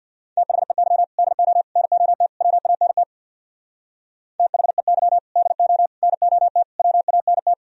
Morse code, heard as a keyed tone, about 60 wpm, 700 Hz. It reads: THEY DO NOT WANT THEY DO NOT WANT
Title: Speed-Racing - Sets of 4 Words 50wpm